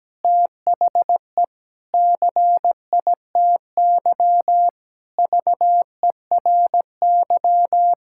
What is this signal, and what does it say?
Morse code, heard as a keyed tone, about 17 wpm, 700 Hz. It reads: THE CITY VERY
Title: Sentences from Top 1000 Words - Review Part 6 - 17wpm